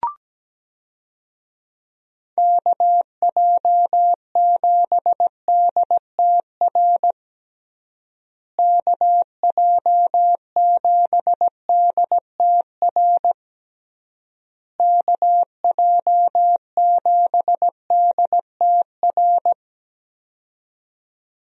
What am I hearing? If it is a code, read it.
Morse code, heard as a keyed tone, about 17 wpm, 700 Hz. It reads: KJ7DTR KJ7DTR KJ7DTR
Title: USA Call Signs - 6 Characters - Repeated 3x - 17wpm